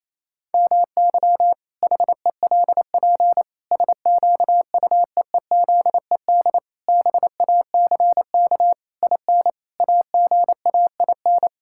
Morse code, heard as a keyed tone, about 28 wpm, 700 Hz. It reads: MY HELP SQUEEZED BACK IN AGAIN
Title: Pangram Challenge v2 - 28wpm